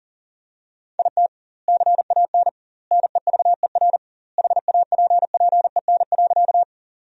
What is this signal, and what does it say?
Morse code, heard as a keyed tone, about 40 wpm, 700 Hz. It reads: IT CAN NEVER HAPPEN.
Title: 2x Word-Spacing - Sentences from Top 400 Words - 40wpm